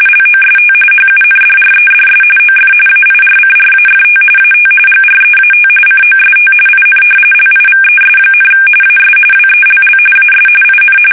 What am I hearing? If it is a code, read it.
RTTY, about 45.45 baud. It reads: LOST HUNDREDS OF AIRCRAFT AT GREAT COST IN LIVES, AS WELL AS IN
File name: fsk850 / battleshipnj